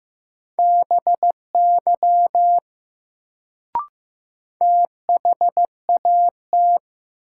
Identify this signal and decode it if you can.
Morse code, heard as a keyed tone, about 15 wpm, 700 Hz. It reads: BY THAT
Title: Sentences from Top 100 Words - New Word Drill - 15wpm